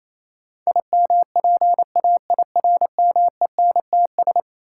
Morse code, heard as a keyed tone, about 28 wpm, 700 Hz. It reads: IMPAIRMENTS
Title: Sets of 5 Words Encore - New Word Drill - 28wpm